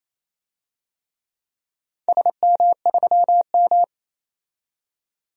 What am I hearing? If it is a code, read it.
Morse code, heard as a keyed tone, about 28 wpm, 690 Hz. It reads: SM3M
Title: DXCC-100 Call Signs - 4 Characters - 28wpm